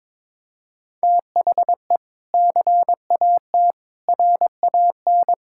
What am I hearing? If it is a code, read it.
Morse code, heard as a keyed tone, about 22 wpm, 700 Hz. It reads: THE CAT RAN
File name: practice / 22wpm / Sentences with 2-Word Phrases - 22wpm